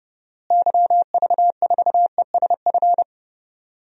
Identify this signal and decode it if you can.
Morse code, heard as a keyed tone, about 30 wpm, 700 Hz. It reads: YV4ESF
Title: DXCC-100 Call Signs - 6 Characters - Repeated 3x - 30wpm